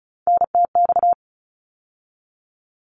Morse code, heard as a keyed tone, about 34 wpm, 700 Hz. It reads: NTX
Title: ARRL Sections 35wpm